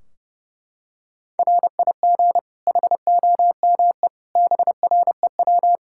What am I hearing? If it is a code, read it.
Morse code, heard as a keyed tone, about 30 wpm, 700 Hz. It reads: RIG HOME BREW
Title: K1USN SST QSO Practice 30wpm